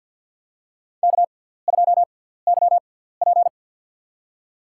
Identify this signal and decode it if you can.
Morse code, heard as a keyed tone, about 53 wpm, spaced as for 17 wpm, 700 Hz. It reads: K2YP
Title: USA Call Signs - 4 Characters - 6x Character Spacing - 50wpm